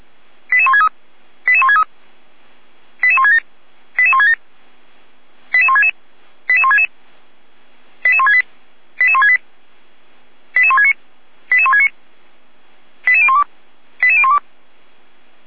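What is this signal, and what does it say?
Five-tone selective call (ZVEI), heard as quick runs of five tones, tones of about 80 ms. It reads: 88163 88163 88167 88167 88166 88166 88168 88168 88169 88169 88012 88012
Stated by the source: ZVEI five-tone selective call received by SDR.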